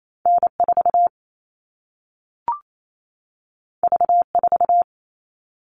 Morse code, heard as a keyed tone, about 28 wpm, 700 Hz. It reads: N4 V4
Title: All Cut Numbers - 2-Digit 28wpm